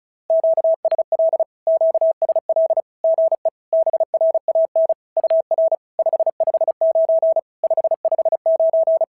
Morse code, heard as a keyed tone, about 36 wpm, 650 Hz. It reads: QSL QSL GE BRAN UR 559 559
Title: SOTA QSO Practice 35wpm